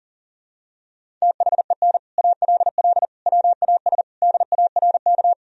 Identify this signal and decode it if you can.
Morse code, heard as a keyed tone, about 40 wpm, 700 Hz. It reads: THEN ALL WAS DARK
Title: Sentences from Top 500 Words and Abbreviations 40wpm